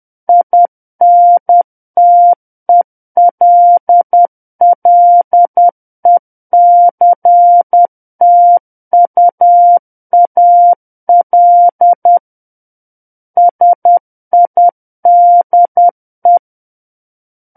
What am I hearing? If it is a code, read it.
Morse code, heard as a keyed tone, about 10 wpm, 700 Hz. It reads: INTELLECTUAL SIDE
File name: Book One / 01-Chapter 1 - 10 WPM